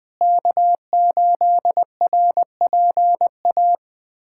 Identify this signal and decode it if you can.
Morse code, heard as a keyed tone, about 20 wpm, 700 Hz. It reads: K8RPA
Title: USA Callsign DE Callsign - 5 Characters - 20wpm